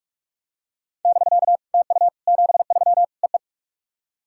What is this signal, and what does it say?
Morse code, heard as a keyed tone, about 44 wpm, 690 Hz. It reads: <BK> TU 73 EE
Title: SOTA QSO Practice 45wpm